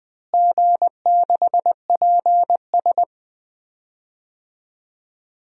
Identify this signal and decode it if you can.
Morse code, heard as a keyed tone, about 21 wpm, 700 Hz. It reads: G6PS
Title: DXCC-100 Call Signs - 4 Characters - Repeated 2x - 20wpm